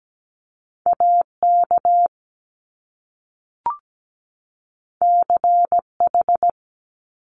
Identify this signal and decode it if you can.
Morse code, heard as a keyed tone, about 17 wpm, 700 Hz. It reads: AK CH
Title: Top Two Letter Combinations in Words 17wpm